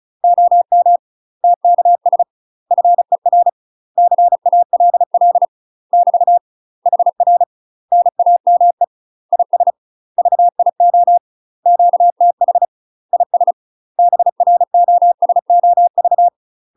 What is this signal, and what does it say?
Morse code, heard as a keyed tone, about 36 wpm, 700 Hz. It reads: OM TKS FER CALL <BT> HR NAME IS VIO QTH IS BROSOV
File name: UK MASTER QSOs TRAINING / 35WPM / UK MASTER QSOs 0028